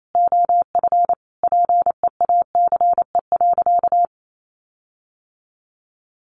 Morse code, heard as a keyed tone, about 28 wpm, 700 Hz. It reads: OF PEACE.
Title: Sentences from Top 1500 Words 28wpm